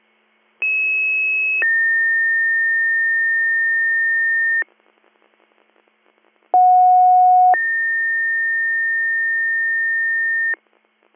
Two-tone page, heard as a one-second tone, then a three-second tone, about 2600 then 1800 Hz, about 730 then 1800 Hz. It is heard from 0.6 to 10.5 s.